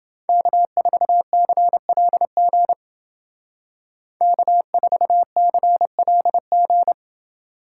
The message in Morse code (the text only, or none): K4CLG K4CLG